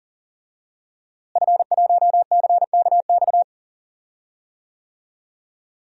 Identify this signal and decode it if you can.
Morse code, heard as a keyed tone, about 40 wpm, 700 Hz. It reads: F1CKX
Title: DXCC-100 Call Signs - 5 Characters - Repeated 2x - 40wpm